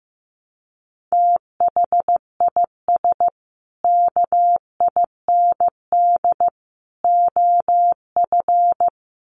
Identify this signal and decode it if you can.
Morse code, heard as a keyed tone, about 15 wpm, 700 Hz. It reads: THIS KIND OF
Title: Sets of 3 Words 15wpm